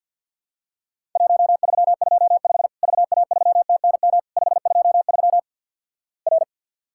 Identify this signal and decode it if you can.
Morse code, heard as a keyed tone, about 50 wpm, 700 Hz. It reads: 1325 VA3TNM 523 R
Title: Worked All Europe DX Contest - QTC Exchange - 50wpm